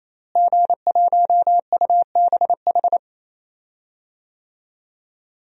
Morse code, heard as a keyed tone, about 28 wpm, 700 Hz. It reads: G1UBH
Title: DXCC-100 Call Signs - 5 Characters - Repeated 3x - 28wpm